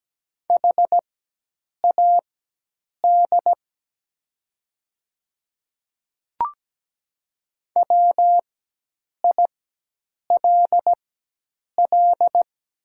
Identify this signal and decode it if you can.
Morse code, heard as a keyed tone, about 17 wpm, spaced as for 8 wpm, 700 Hz. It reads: HAD WILL
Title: 4x Character Spacing - Top 200 Words - 17wpm